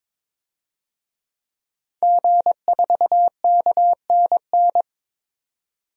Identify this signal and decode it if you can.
Morse code, heard as a keyed tone, about 22 wpm, 700 Hz. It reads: G4KNN
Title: DXCC-100 Call Signs - 5 Characters - 22wpm